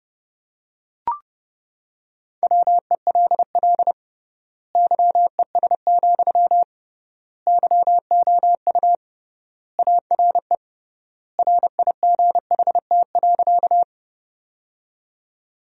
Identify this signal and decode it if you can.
Morse code, heard as a keyed tone, about 30 wpm, 700 Hz. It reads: WELL YES, YOU ARE RIGHT.
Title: 3x Word-Spacing - Sentences from Top 500 Words - 30wpm